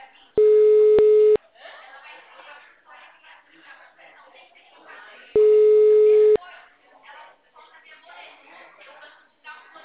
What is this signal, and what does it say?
Ringback tone (Europe, 425 Hz), with speech.
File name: out- -2014- - - .4